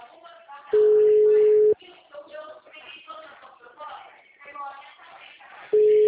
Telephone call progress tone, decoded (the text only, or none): ringback tone